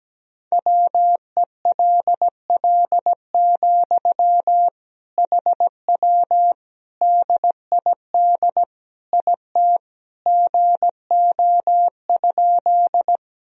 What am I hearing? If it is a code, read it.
Morse code, heard as a keyed tone, about 17 wpm, 700 Hz. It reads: WELL, HW DID IT GO?
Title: Sentences from Top 200 Words and Abbreviations 17wpm